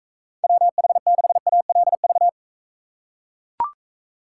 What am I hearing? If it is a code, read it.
Morse code, heard as a keyed tone, about 44 wpm, 700 Hz. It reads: WH6ALV